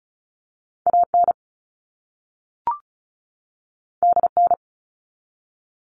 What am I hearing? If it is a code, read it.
Morse code, heard as a keyed tone, about 36 wpm, 700 Hz. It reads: AN DN
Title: All Cut Numbers - 2-Digit 35wpm